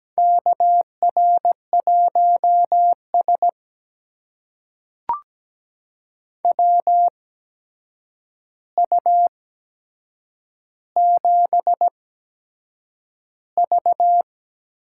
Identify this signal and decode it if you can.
Morse code, heard as a keyed tone, about 17 wpm, spaced as for 7 wpm, 700 Hz. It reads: KR1S WU7V
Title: USA Call Signs - 4 Characters - 8x Character Spacing - 17wpm